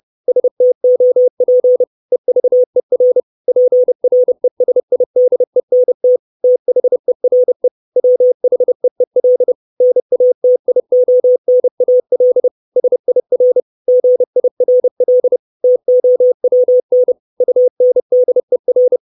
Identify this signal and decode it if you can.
Morse code, heard as a keyed tone, about 30 wpm, 500 Hz. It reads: STOP EVER PRESIDENT THERE WHEEL NATIONAL SIR GIRL TOWN UNDER
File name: 632 Words 30WPM